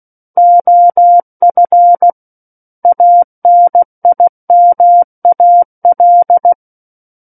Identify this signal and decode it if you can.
Morse code, heard as a keyed tone, about 16 wpm, 700 Hz. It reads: OF ANIMAL
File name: Book One / 07-Chapter 7 - 16 WPM